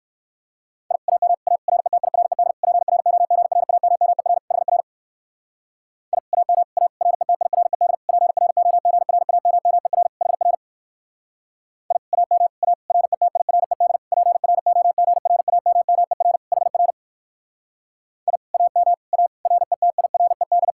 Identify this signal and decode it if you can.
Morse code, heard as a keyed tone, about 59 wpm, 700 Hz. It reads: I AM A RETIRED PROGRAMMER HR I AM A RETIRED PROGRAMMER HR I AM A RETIRED PROGRAMMER HR I AM A RETIRED